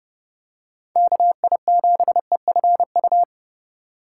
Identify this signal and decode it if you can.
Morse code, heard as a keyed tone, about 31 wpm, 700 Hz. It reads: KI7EFU
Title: USA Call Signs - 6 Characters - 4x Character Spacing - 30wpm